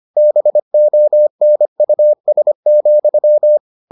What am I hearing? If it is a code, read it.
Morse code, heard as a keyed tone, about 25 wpm, 600 Hz. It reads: BONUS,